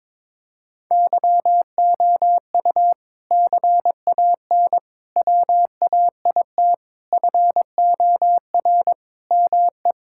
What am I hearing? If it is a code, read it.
Morse code, heard as a keyed tone, about 21 wpm, 700 Hz. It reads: YOU CAN WAIT FOR ME
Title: Sentences from Top 1000 Words - Review Part 6 - 22wpm